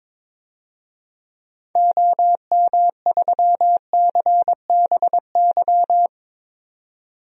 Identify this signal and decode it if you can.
Morse code, heard as a keyed tone, about 21 wpm, 700 Hz. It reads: OM3CBY
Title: DXCC-100 Call Signs - 6 Characters - 22wpm